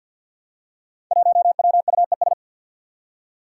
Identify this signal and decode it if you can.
Morse code, heard as a keyed tone, about 52 wpm, 700 Hz. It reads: 1WUES
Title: Five Challenging Letter-Number Combinations 50wpm